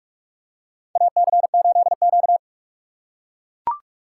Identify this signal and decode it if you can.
Morse code, heard as a keyed tone, about 41 wpm, 700 Hz. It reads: AC8Q
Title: USA Call Signs - 4 Characters - 4x Character Spacing - 45wpm